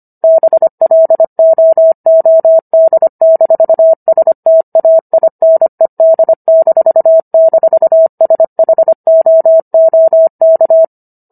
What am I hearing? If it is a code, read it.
Morse code, heard as a keyed tone, about 25 wpm, 650 Hz. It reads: BLOOD-STAINED--SHOOK